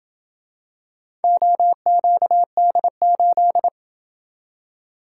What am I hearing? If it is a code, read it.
Morse code, heard as a keyed tone, about 27 wpm, 700 Hz. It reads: OQD8